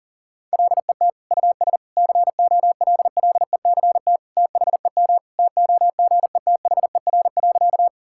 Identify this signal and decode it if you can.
Morse code, heard as a keyed tone, about 40 wpm, 700 Hz. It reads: LET US COLLECT THEM TOGETHER.
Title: Sentences from Top 1600 Words 40wpm